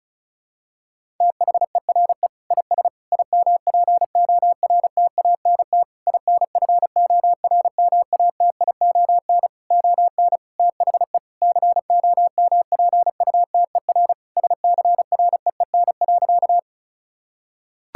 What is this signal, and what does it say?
Morse code, heard as a keyed tone, about 34 wpm, 700 Hz. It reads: THERE IS IMPORTANT INFORMATION ON THE COMPUTER SCREEN.
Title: Sentences from Top 1000 Words - Review Part 6 - 35wpm